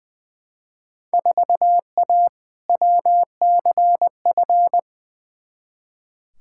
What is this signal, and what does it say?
Morse code, heard as a keyed tone, about 21 wpm, 700 Hz. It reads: 4A WCF
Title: Field Day Exchanges - 20wpm